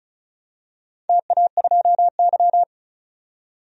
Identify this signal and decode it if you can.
Morse code, heard as a keyed tone, about 36 wpm, 700 Hz. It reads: TA2Y
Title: DXCC-100 Call Signs - 4 Characters - 35wpm